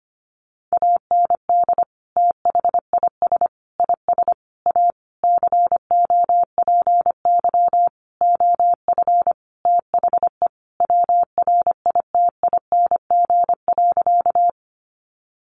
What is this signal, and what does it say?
Morse code, heard as a keyed tone, about 25 wpm, 700 Hz. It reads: AND THIS IS A COPY OF THE WRITING.
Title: Sentences from Top 1400 Words 25wpm